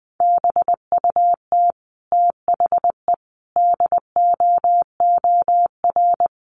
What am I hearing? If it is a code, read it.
Morse code, heard as a keyed tone, about 20 wpm, 700 Hz. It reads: BUT THE DOOR
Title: Sentences from Top 600 Words 20wpm